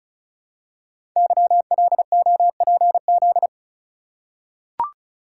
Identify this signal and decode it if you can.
Morse code, heard as a keyed tone, about 36 wpm, 690 Hz. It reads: YLOPZ
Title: Code Groups - 5 Letters - 35wpm